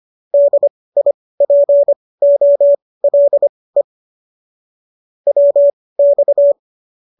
Morse code, heard as a keyed tone, about 25 wpm, spaced as for 18 wpm, 570 Hz. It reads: DIPOLE WX